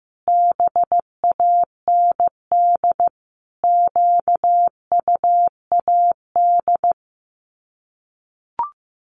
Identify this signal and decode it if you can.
Morse code, heard as a keyed tone, about 15 wpm, 700 Hz. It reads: BAND QUAD
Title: Common QSO Phrases 15wpm